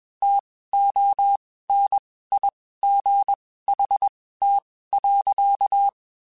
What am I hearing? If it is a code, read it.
Morse code, heard as a keyed tone, about 21 wpm, spaced as for 15 wpm, 800 Hz. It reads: TONIGHT.
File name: (spaced) / QSOs Spaced-mp3s / WD0BGP 1 @20 sp